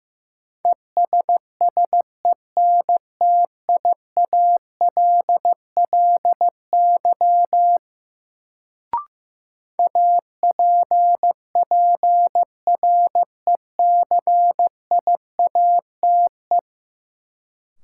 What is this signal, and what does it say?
Morse code, heard as a keyed tone, about 15 wpm, 700 Hz. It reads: ESSENTIALLY APPRECIATE
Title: Sentences from Top 1300 Words - New Word Drill - 15wpm